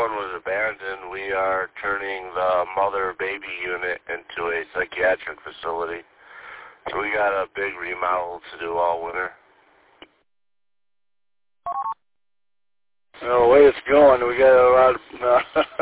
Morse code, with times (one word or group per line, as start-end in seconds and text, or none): none